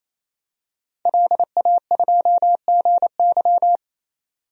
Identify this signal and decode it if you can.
Morse code, heard as a keyed tone, about 28 wpm, 700 Hz. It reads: LA2GY
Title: DXCC-100 Call Signs - 5 Characters - 28wpm